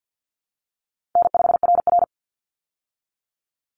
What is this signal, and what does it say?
Morse code, heard as a keyed tone, about 50 wpm, 700 Hz. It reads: N5RR
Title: Easy Call Signs Repeated 2x - 50wpm